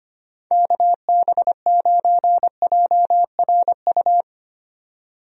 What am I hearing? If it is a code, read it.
Morse code, heard as a keyed tone, about 25 wpm, 700 Hz. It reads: KB9JRU
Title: USA Call Signs - 6 Characters - Repeated 2x - 25wpm